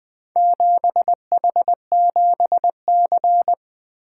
Morse code, heard as a keyed tone, about 20 wpm, 700 Hz. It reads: 7H7C
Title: Four Letter-Number Combinations - 20wpm